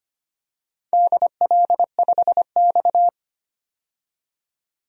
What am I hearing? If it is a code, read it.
Morse code, heard as a keyed tone, about 25 wpm, 700 Hz. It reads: DL5X
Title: DXCC-100 Call Signs - 4 Characters - 25wpm